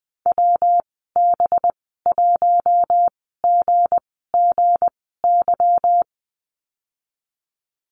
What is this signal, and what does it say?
Morse code, heard as a keyed tone, about 20 wpm, spaced as for 15 wpm, 700 Hz. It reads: WB1GGY
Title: 219 Lesson 35g - Number 0 - Call Signs 20wpm